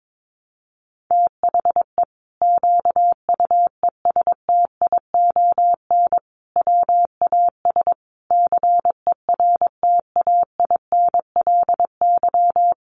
Morse code, heard as a keyed tone, about 22 wpm, 700 Hz. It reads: THE QUESTION WAS CERTAINLY
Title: Sentences from Top 1200 Words 22wpm